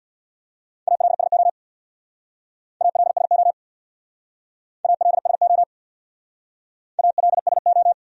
Morse code, heard as a keyed tone, about 61 wpm, 700 Hz. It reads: ALSO ALSO ALSO ALSO
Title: Speed-Racing Top 500 Words 50wpm